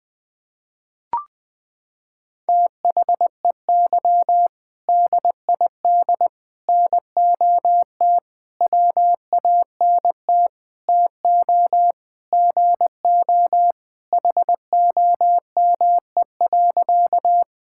Morse code, heard as a keyed tone, about 20 wpm, 700 Hz. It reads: THEY DID NOT WANT TO GO HOME.
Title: Sentences with 5-Word Phrases - 20wpm